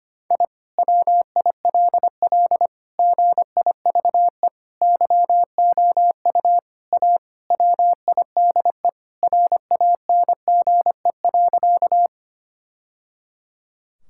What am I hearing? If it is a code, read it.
Morse code, heard as a keyed tone, about 25 wpm, 700 Hz. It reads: I WILL GIVE YOU A WIDE RANGE.
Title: Sentences from Top 1000 Words - Review Part 6 - 25wpm